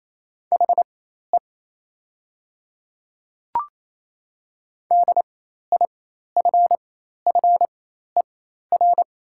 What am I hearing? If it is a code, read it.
Morse code, heard as a keyed tone, about 28 wpm, spaced as for 13 wpm, 700 Hz. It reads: HE DIFFER
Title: 4x Character Spacing - Top 200 Words - 28wpm